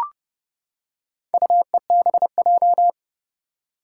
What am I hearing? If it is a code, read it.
Morse code, heard as a keyed tone, about 31 wpm, 700 Hz. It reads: UEBJ
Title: Four Challenging Letter-Number Combinations 30wpm